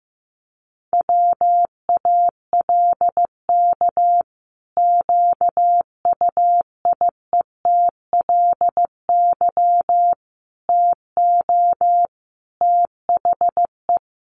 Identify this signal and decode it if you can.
Morse code, heard as a keyed tone, about 15 wpm, 700 Hz. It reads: WALK QUIETLY TO THE
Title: A Sound Of Thunder 15wpm